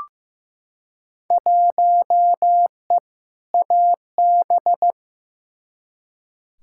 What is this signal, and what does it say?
Morse code, heard as a keyed tone, about 15 wpm, 700 Hz. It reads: 1E AB
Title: Field Day Exchanges - 15wpm